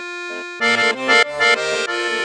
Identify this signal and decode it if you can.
There is no code.